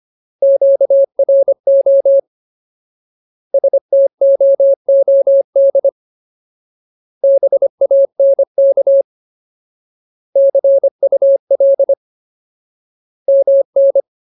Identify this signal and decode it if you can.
Morse code, heard as a keyed tone, about 25 wpm, 550 Hz. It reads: QRO STOOD BANK CUL MN
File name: Ultimate Word List / 25WPM Ultimate